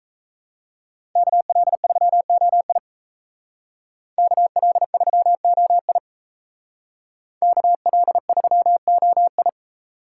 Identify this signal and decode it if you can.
Morse code, heard as a keyed tone, about 34 wpm, 700 Hz. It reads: KL3OI KL3OI KL3OI
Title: Speed Racing - USA 2x2 Call Signs 28wpm